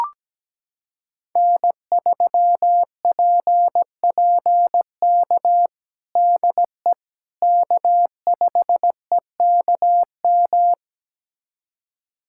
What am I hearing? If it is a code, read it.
Morse code, heard as a keyed tone, about 17 wpm, 700 Hz. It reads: N3PPK DE K5EKM
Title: USA Callsign DE Callsign - 5 Characters - 17wpm